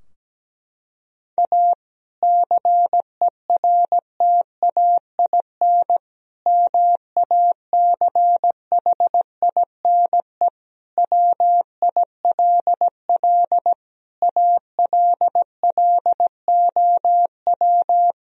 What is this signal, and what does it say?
Morse code, heard as a keyed tone, about 17 wpm, 700 Hz. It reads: A CERTAIN MACHINE WILL ALLOW
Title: Sentences from Top 1000 Words - Review Part 2 - 17wpm